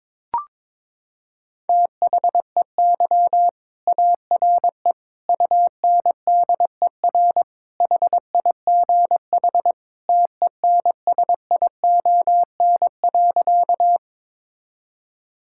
Morse code, heard as a keyed tone, about 22 wpm, 700 Hz. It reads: THEY ARE UNDER HIGH TENSION.
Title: Sentences from Top 1600 Words 22wpm